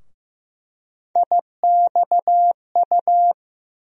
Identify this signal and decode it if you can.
Morse code, heard as a keyed tone, about 16 wpm, 700 Hz. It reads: IXU
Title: USA Call Sign Postfixes - 3 Letters 15wpm